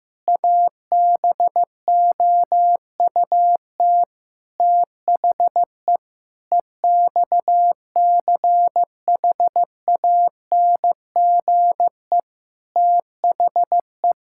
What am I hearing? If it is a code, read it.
Morse code, heard as a keyed tone, about 15 wpm, 700 Hz. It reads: ABOUT THE EXCHANGE THE
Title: Sentences from Top 1000 Words - Review Part 6 - 15wpm